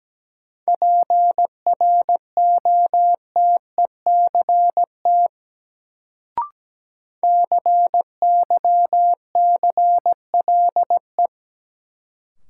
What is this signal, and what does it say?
Morse code, heard as a keyed tone, about 17 wpm, 700 Hz. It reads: PROTECT CYCLE
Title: Sentences from Top 900 Words - New Word Drill - 17wpm